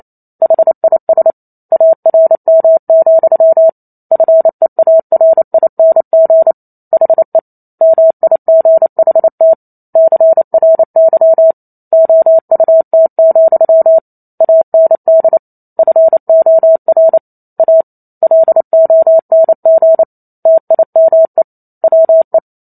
Morse code, heard as a keyed tone, about 28 wpm, 650 Hz. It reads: HIS ARM, FEARING HE MIGHT CRY OUT, AND FOR A LONG TIME WE